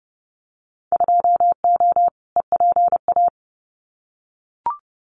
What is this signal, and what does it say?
Morse code, heard as a keyed tone, about 30 wpm, 700 Hz. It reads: 2O EPA
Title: Winter Field Day Exchanges Encore 30wpm